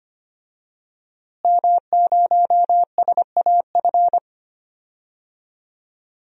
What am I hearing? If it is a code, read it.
Morse code, heard as a keyed tone, about 25 wpm, 700 Hz. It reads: M0SAF
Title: DXCC-100 Call Signs - 5 Characters - Repeated 2x - 25wpm